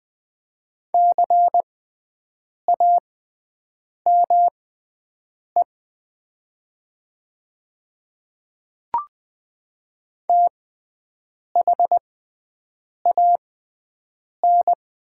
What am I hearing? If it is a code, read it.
Morse code, heard as a keyed tone, about 20 wpm, spaced as for 7 wpm, 700 Hz. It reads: CAME THAN
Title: 6x Character Spacing - Top 200 Words - 20wpm